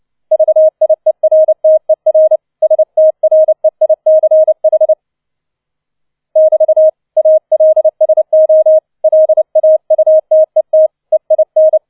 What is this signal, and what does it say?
Morse code, heard as a keyed tone, about 29 wpm, 630 Hz. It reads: VIERTER STREICH <BT> ALSO LAUTET EIN
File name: 004 Vierter Streich 30wpm